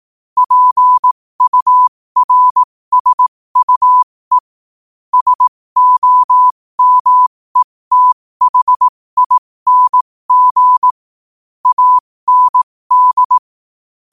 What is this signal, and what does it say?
Morse code, heard as a keyed tone, about 18 wpm, spaced as for 15 wpm, 1000 Hz. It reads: PURSUE SOMETHING AND
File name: qotdfiles / 15wpm